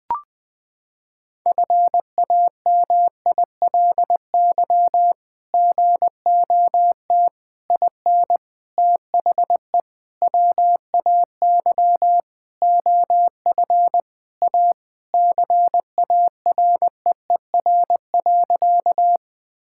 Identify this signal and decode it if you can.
Morse code, heard as a keyed tone, about 20 wpm, 700 Hz. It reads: FAMILY GOT IN THE WAY OF A CAREER.
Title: Sentences from Top 1000 Words - Review Part 6 - 20wpm